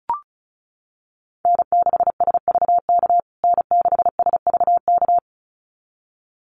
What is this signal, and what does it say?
Morse code, heard as a keyed tone, about 34 wpm, 700 Hz. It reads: N6SVK N6SVK
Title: USA Call Signs Repeated 2x 35wpm